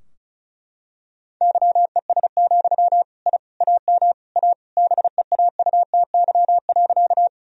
Morse code, heard as a keyed tone, about 34 wpm, 700 Hz. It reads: YES, I AM A BEAUTY.
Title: Sentences from Top 1000 Words - Review Part 2 - 35wpm